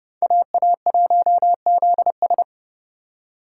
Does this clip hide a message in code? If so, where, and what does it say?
Yes: Morse code from 0.2 to 2.4 s, heard as a keyed tone, about 28 wpm, 700 Hz. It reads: AA1ZS